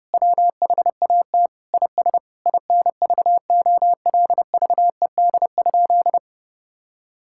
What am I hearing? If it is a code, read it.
Morse code, heard as a keyed tone, about 30 wpm, 700 Hz. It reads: WHAT IS INVOLVED?
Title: Sentences from Top 1000 Words - Review Part 5 - 30wpm